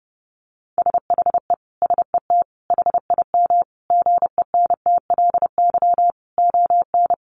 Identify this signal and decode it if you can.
Morse code, heard as a keyed tone, about 30 wpm, 700 Hz. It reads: SHE SET HIM GENTLY ON
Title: Sentences from Top 1500 Words 30wpm